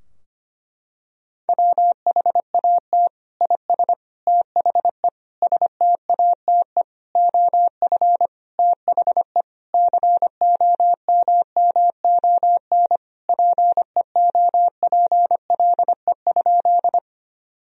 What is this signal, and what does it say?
Morse code, heard as a keyed tone, about 25 wpm, 700 Hz. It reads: WHAT IS THE STATE OF THE COMMON PEOPLE?
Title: Sentences from Top 1000 Words - Review Part 3 - 25wpm